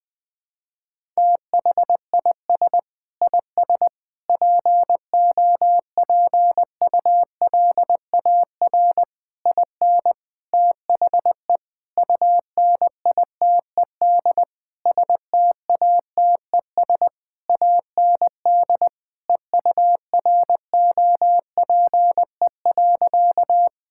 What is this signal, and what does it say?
Morse code, heard as a keyed tone, about 21 wpm, 700 Hz. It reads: THIS IS POPULAR IN THE UNITED STATES AND EUROPE.
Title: Sentences with 5-Word Phrases - Spoken Beginning - 20wpm